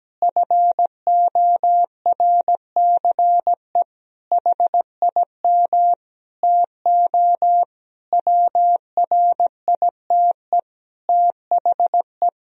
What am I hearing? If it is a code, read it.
Morse code, heard as a keyed tone, about 17 wpm, 700 Hz. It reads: FORCE HIM TO WRITE THE
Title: Sentences from Top 1000 Words - Review Part 3 - 17wpm